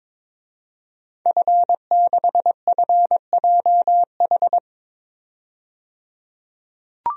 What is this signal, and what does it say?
Morse code, heard as a keyed tone, about 22 wpm, 700 Hz. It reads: F6FJH
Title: DXCC-100 Call Signs - 5 Characters - 22wpm